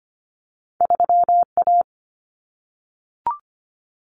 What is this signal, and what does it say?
Morse code, heard as a keyed tone, about 25 wpm, 700 Hz. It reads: 3A
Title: Common Cut Numbers - 2-Digit 25wpm